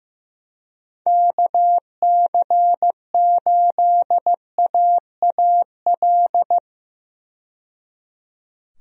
Morse code, heard as a keyed tone, about 15 wpm, 700 Hz. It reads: KC8AAL
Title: USA Call Signs - 6 Characters - 15wpm